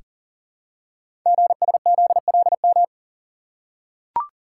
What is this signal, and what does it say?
Morse code, heard as a keyed tone, about 39 wpm, 700 Hz. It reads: GSZLM
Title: Code Groups - 5 Letters - 40wpm